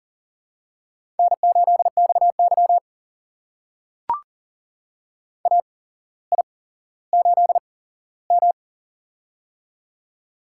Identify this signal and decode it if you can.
Morse code, heard as a keyed tone, about 39 wpm, spaced as for 17 wpm, 700 Hz. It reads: N8XY AI8M
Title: USA Call Signs - 4 Characters - 8x Character Spacing - 40wpm